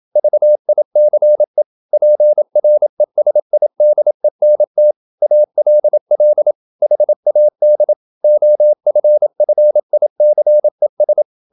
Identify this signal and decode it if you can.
Morse code, heard as a keyed tone, about 27 wpm, 600 Hz. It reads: VICE PRESIDENT ALL HAD OFFICES
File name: Never 0027wpm